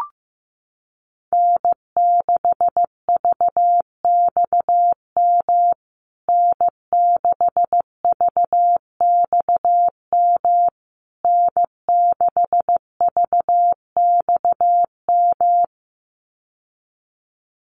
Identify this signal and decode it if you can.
Morse code, heard as a keyed tone, about 15 wpm, 700 Hz. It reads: N6VXM N6VXM N6VXM
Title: USA Call Signs Repeated 3x 15wpm